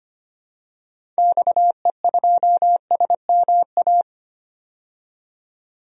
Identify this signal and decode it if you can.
Morse code, heard as a keyed tone, about 25 wpm, 700 Hz. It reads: XE2SMA
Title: DXCC-100 Call Signs - 6 Characters - 25wpm